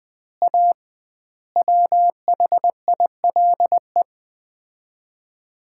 Morse code, heard as a keyed tone, about 21 wpm, 700 Hz. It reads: A WHILE
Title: 2x Word-Spacing - Sets of 5 Words - 20wpm